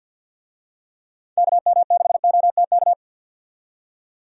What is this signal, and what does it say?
Morse code, heard as a keyed tone, about 50 wpm, 700 Hz. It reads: KM6OTK
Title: USA Call Signs - 6 Characters - 50wpm